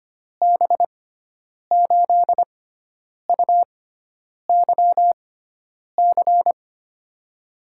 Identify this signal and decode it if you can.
Morse code, heard as a keyed tone, about 25 wpm, spaced as for 9 wpm, 700 Hz. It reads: B8UYC
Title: USA Call Signs - 6 Characters - 6x Character Spacing - 25wpm